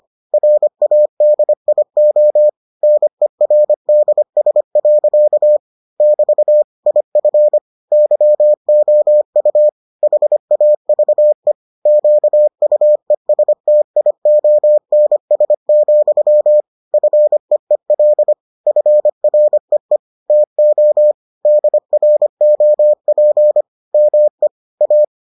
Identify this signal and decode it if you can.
Morse code, heard as a keyed tone, about 25 wpm, 600 Hz. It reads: RADIO NERDS. <BT> IF YOU HAVE QUESTIONS, FEEL FREE TO DROP ME A